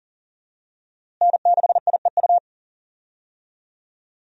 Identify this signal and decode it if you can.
Morse code, heard as a keyed tone, about 40 wpm, 700 Hz. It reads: N6IEU
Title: USA Call Signs - 5 Characters - Repeated 2x - 40wpm